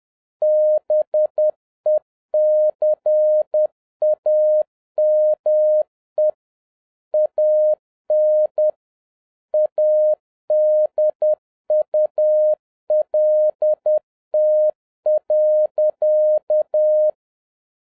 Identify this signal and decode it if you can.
Morse code, heard as a keyed tone, about 10 wpm, 610 Hz. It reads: BECAME AN ADULT.